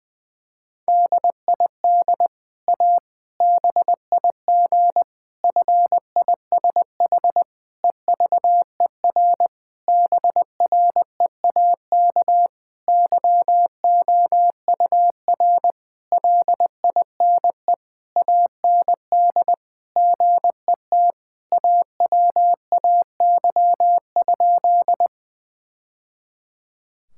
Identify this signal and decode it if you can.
Morse code, heard as a keyed tone, about 20 wpm, 700 Hz. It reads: DID A BIG FISH EVER BREAK YOUR LINE AND GET AWAY?
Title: Sentences from Top 1000 Words - Review Part 6 - 20wpm